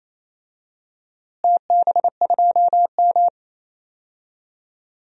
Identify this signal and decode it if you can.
Morse code, heard as a keyed tone, about 28 wpm, 700 Hz. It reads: TB2M
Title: DXCC-100 Call Signs - 4 Characters - Repeated 2x - 28wpm